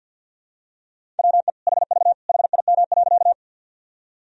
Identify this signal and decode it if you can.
Morse code, heard as a keyed tone, about 52 wpm, 700 Hz. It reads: WE HV HIM.
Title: Sentences from Top 100 Words and Abbreviations 50wpm